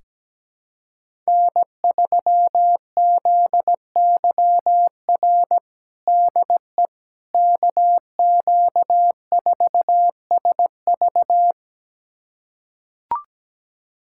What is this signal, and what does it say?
Morse code, heard as a keyed tone, about 17 wpm, 700 Hz. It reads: N3ZYR DE KQ4SV
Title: USA Callsign DE Callsign - 5 Characters - 17wpm